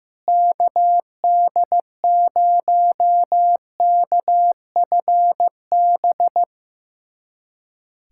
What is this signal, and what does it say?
Morse code, heard as a keyed tone, about 15 wpm, 700 Hz. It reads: KD0KFB
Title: USA Callsign DE Callsign - 6 Characters - 15wpm